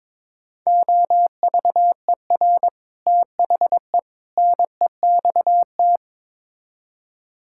Morse code, heard as a keyed tone, about 22 wpm, 700 Hz. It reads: OVER THE NEXT
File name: practice / 22wpm / Build A Phrase - 5-Word Phrases - 22wpm